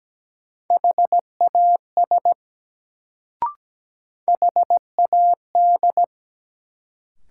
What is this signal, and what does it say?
Morse code, heard as a keyed tone, about 17 wpm, 700 Hz. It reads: HAS HAD
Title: Sets of 2 Words - New Word Drill - 17wpm